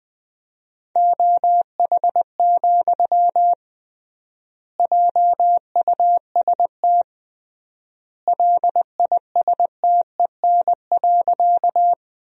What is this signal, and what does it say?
Morse code, heard as a keyed tone, about 21 wpm, 700 Hz. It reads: OH, JUST LISTEN.
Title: 3x Word-Spacing - Sentences from Top 500 Words - 20wpm